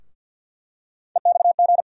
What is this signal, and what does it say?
Morse code, heard as a keyed tone, about 49 wpm, 700 Hz. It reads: EXG
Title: USA Call Sign Postfixes - 3 Letters 50wpm